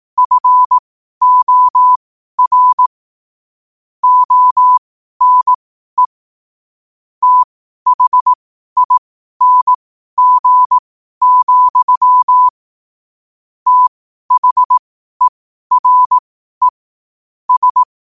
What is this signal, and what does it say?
Morse code, heard as a keyed tone, about 18 wpm, spaced as for 12 wpm, 1000 Hz. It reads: FOR ONE THING, THERE S